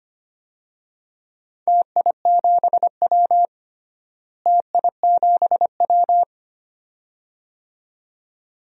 Morse code, heard as a keyed tone, about 25 wpm, 700 Hz. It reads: TI7W TI7W
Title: DXCC-100 Call Signs - 4 Characters - Repeated 2x - 25wpm